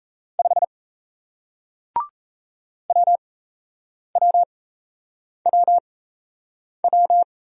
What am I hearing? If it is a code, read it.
Morse code, heard as a keyed tone, about 38 wpm, spaced as for 8 wpm, 700 Hz. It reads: 5 WWWW